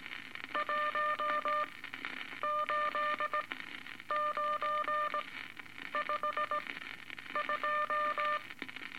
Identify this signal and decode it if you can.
Morse code, heard as a keyed tone, about 18 wpm, spaced as for 9 wpm, 1200 Hz, with noise. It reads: 18952
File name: Eight Words Per Minute